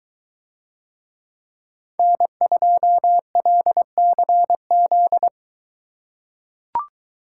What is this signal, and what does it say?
Morse code, heard as a keyed tone, about 21 wpm, 700 Hz. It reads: N2LCZ